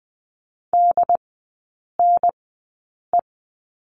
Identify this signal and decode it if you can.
Morse code, heard as a keyed tone, about 20 wpm, spaced as for 8 wpm, 700 Hz. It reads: DNE
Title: 067 Lesson 12e - Letter d - Three Characters 20wpm